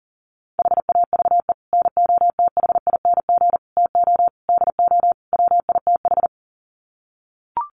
Morse code, heard as a keyed tone, about 40 wpm, 700 Hz. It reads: HAVE NOTHING TO DO WITH
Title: Sets of 5 Words 40wpm